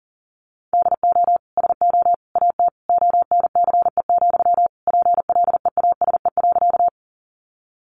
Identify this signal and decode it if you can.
Morse code, heard as a keyed tone, about 40 wpm, 700 Hz. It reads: DO SO AT ONCE, PLEASE.
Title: Sentences from Top 1500 Words 40wpm